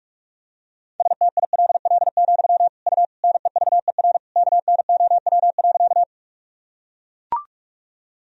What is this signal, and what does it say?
Morse code, heard as a keyed tone, about 41 wpm, 700 Hz. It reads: STILL, U NEVER KNOW.
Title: Sentences from Top 300 Words and Abbreviations 45wpm